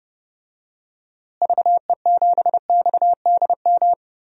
Morse code, heard as a keyed tone, about 30 wpm, 700 Hz. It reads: VE7XDM
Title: DXCC-100 Call Signs - 6 Characters - Repeated 3x - 30wpm